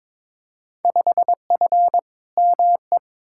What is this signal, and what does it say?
Morse code, heard as a keyed tone, about 22 wpm, 700 Hz. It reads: 5F ME
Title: Field Day Exchanges Encore - 22wpm